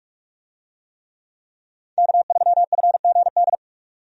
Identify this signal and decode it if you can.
Morse code, heard as a keyed tone, about 46 wpm, 700 Hz. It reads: K3FGD
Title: USA Call Signs - 5 Characters - 45wpm